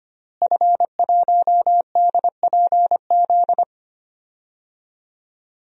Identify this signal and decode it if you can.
Morse code, heard as a keyed tone, about 25 wpm, 700 Hz. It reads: F1DPZ
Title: DXCC-100 Call Signs - 5 Characters - Repeated 3x - 25wpm